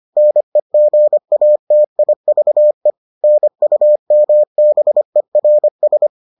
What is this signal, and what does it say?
Morse code, heard as a keyed tone, about 25 wpm, 600 Hz. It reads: NEGATIVE NUMBERS